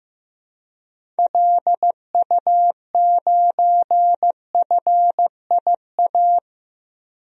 Morse code, heard as a keyed tone, about 16 wpm, 700 Hz. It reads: LU9FIA
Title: DXCC-100 Call Signs - 6 Characters - Repeated 3x - 15wpm